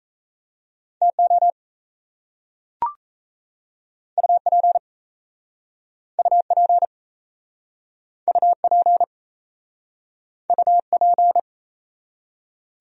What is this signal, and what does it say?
Morse code, heard as a keyed tone, about 37 wpm, 700 Hz. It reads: TO UP UP UP UP